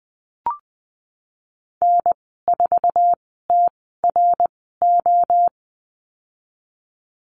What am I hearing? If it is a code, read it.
Morse code, heard as a keyed tone, about 19 wpm, spaced as for 14 wpm, 700 Hz. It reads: N4TRO
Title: 055 Lesson 10g - Letter r - Call Signs 20wpm